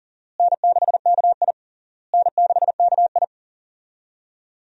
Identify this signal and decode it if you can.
Morse code, heard as a keyed tone, about 40 wpm, 700 Hz. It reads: N6KI N6KI
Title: USA Call Signs - 4 Characters - Repeated 3x - 40wpm